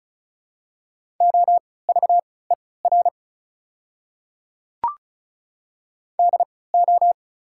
Morse code, heard as a keyed tone, about 36 wpm, spaced as for 20 wpm, 690 Hz. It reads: OVER DO
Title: 3x Character Spacing - Top 100 Words - 35wpm